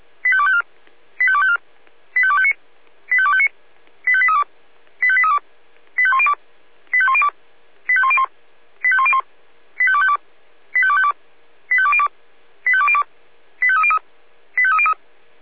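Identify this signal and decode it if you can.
Five-tone selective call (ZVEI), heard as quick runs of five tones, tones of about 70 ms. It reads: 85254 85254 85259 85259 85832 85832 85102 85102 85101 85101 85252 85252 85202 85202 85303 85303
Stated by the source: ZVEI five-tone selective call received on an SDR.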